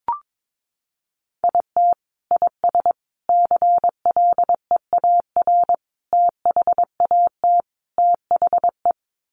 Morse code, heard as a keyed tone, about 22 wpm, 700 Hz. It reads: IT IS CLEAR THAT THE
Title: Sets of 5 Words Encore 22wpm